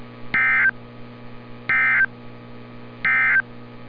Emergency Alert System (SAME) tones, heard as harsh buzzing bursts. It is heard from 0.2 to 3.5 s.
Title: NOAA Weather Radio - Altoona, PA